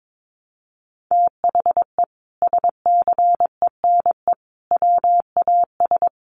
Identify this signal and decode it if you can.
Morse code, heard as a keyed tone, about 22 wpm, 700 Hz. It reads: THE SCENE WAS
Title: Sentences from Top 1100 Words 22wpm